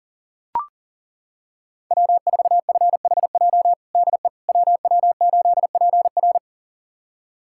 Morse code, heard as a keyed tone, about 40 wpm, 700 Hz. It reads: W4FHJ DE WW8PR
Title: USA Callsign DE Callsign - 5 Characters - 40wpm